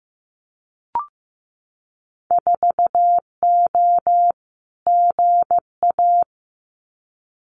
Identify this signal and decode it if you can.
Morse code, heard as a keyed tone, about 15 wpm, 700 Hz. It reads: 4O GA
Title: Winter Field Day Exchanges 15wpm